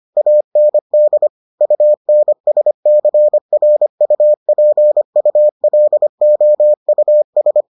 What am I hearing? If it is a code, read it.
Morse code, heard as a keyed tone, about 25 wpm, 600 Hz. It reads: AND UNSCRUPULOUS